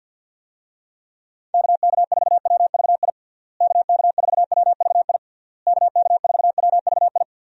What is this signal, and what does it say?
Morse code, heard as a keyed tone, about 49 wpm, 700 Hz. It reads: KK4WVI KK4WVI KK4WVI
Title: USA Call Signs - 6 Characters - Repeated 3x - 50wpm